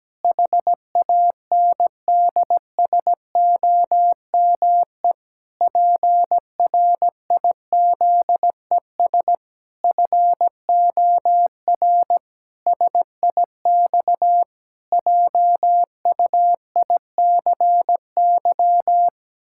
Morse code, heard as a keyed tone, about 17 wpm, 700 Hz. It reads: HANDSOME PRIZES FOR SIX JUICY
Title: Pangram Challenge v2 - 17wpm